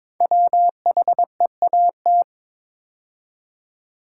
Morse code, heard as a keyed tone, about 22 wpm, 700 Hz. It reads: WHEAT
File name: practice / 22wpm / 5 Letter Words - First 3 Letters Given - 22wpm